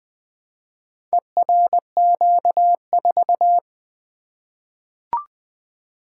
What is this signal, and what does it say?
Morse code, heard as a keyed tone, about 20 wpm, 700 Hz. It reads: ERQ4
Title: Four Letter-Number Combinations - 20wpm